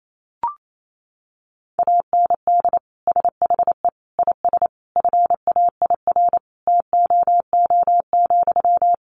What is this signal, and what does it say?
Morse code, heard as a keyed tone, about 28 wpm, 700 Hz. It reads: AND SHE IS FAIR TOO,
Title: Sentences from Top 1300 Words 28wpm